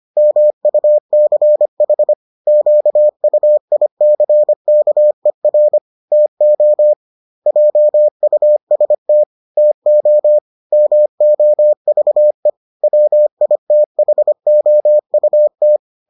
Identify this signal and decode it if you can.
Morse code, heard as a keyed tone, about 25 wpm, 600 Hz. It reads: MUCH QUICKER TO JUST TO MOVE WITHOUT